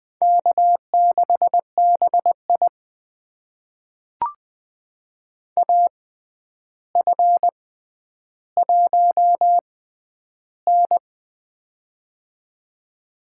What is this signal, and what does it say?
Morse code, heard as a keyed tone, about 20 wpm, spaced as for 10 wpm, 700 Hz. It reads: K6BI AF1N
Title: USA Call Signs - 4 Characters - 6x Character Spacing - 20wpm